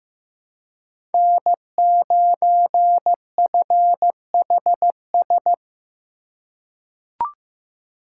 Morse code, heard as a keyed tone, about 15 wpm, 700 Hz. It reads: N9FHS
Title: USA Call Signs - 5 Characters - 3x Character Spacing - 15wpm